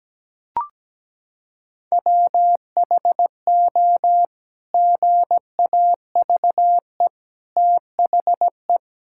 Morse code, heard as a keyed tone, about 17 wpm, 700 Hz. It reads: WHO GAVE THE
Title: Sentences from Top 1000 Words - Review Part 6 - 17wpm